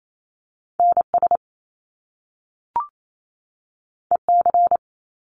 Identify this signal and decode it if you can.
Morse code, heard as a keyed tone, about 28 wpm, 700 Hz. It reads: N S E C
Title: Top Two Letter Combinations in Words 28wpm